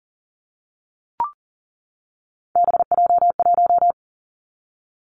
Morse code, heard as a keyed tone, about 40 wpm, 700 Hz. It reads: BJ1
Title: Three Challenging Letter-Number Combinations 40wpm